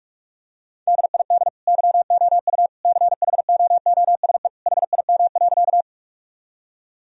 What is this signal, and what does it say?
Morse code, heard as a keyed tone, about 46 wpm, 700 Hz. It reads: DID YOU CHOOSE HIM.
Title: Sentences from Top 1000 Words - Review Part 4 - 45wpm